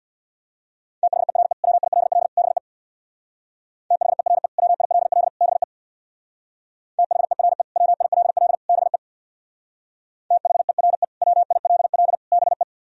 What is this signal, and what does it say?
Morse code, heard as a keyed tone, about 64 wpm, 700 Hz. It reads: THERE WILL BE THERE WILL BE THERE WILL BE THERE WILL BE